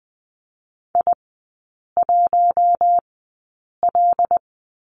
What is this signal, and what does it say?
Morse code, heard as a keyed tone, about 20 wpm, spaced as for 8 wpm, 700 Hz. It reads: I1L
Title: 172 Lesson 28e - Number 9 - Three Characters 20wpm